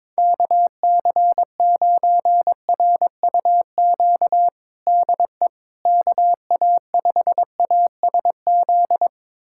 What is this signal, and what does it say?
Morse code, heard as a keyed tone, about 22 wpm, 700 Hz. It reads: KC9RUQ DE KA5ASZ
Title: USA Callsign DE Callsign - 6 Characters - 22wpm